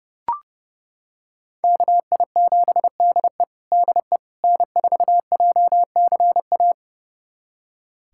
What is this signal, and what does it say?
Morse code, heard as a keyed tone, about 30 wpm, 700 Hz. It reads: KI7DE DE N4JCA
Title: USA Callsign DE Callsign - 5 Characters - 30wpm